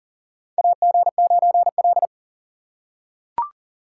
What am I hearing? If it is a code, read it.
Morse code, heard as a keyed tone, about 40 wpm, 700 Hz. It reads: AG9L
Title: USA Call Signs - 4 Characters - 2x Character Spacing - 40wpm